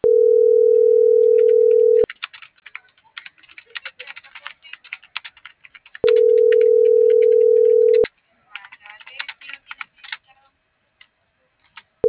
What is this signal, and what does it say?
Ringback tone (North America), 440 + 480 Hz.